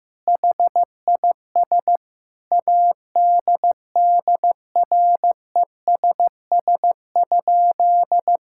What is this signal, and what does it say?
Morse code, heard as a keyed tone, about 15 wpm, 700 Hz. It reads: HIS ADDRESS?
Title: Sentences from Top 1000 Words - Review Part 6 - 15wpm